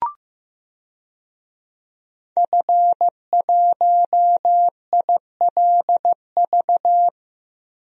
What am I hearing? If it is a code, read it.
Morse code, heard as a keyed tone, about 15 wpm, 700 Hz. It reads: F1ILV
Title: DXCC-100 Call Signs - 5 Characters - Repeated 3x - 15wpm